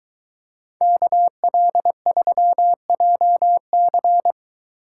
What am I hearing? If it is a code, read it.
Morse code, heard as a keyed tone, about 22 wpm, 700 Hz. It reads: KL3JC